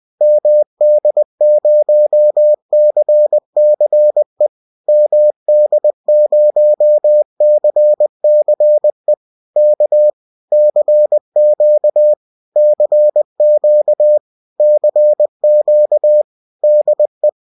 Morse code, heard as a keyed tone, about 20 wpm, 600 Hz. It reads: MD0CCE MD0CCE K CQ CQ CQ DE